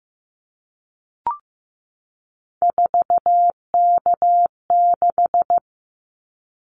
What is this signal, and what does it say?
Morse code, heard as a keyed tone, about 15 wpm, 700 Hz. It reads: 4K6
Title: Callsign Prefixes for Top 100 DXCC - 15wpm